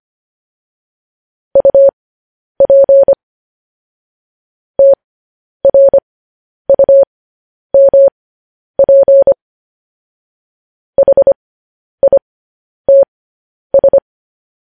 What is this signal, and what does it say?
Morse code, heard as a keyed tone, about 25 wpm, spaced as for 10 wpm, 550 Hz. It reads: UP TRUMP HITS
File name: podcast / News headlines-1x-25x10